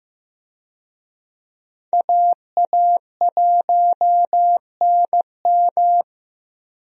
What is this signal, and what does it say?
Morse code, heard as a keyed tone, about 15 wpm, 700 Hz. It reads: AA1NM
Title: USA Call Signs - 5 Characters - Repeated 3x - 15wpm